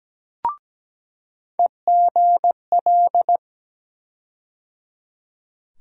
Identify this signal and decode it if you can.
Morse code, heard as a keyed tone, about 17 wpm, 700 Hz. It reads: EGL
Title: USA Call Sign Postfixes - 3 Letters 17wpm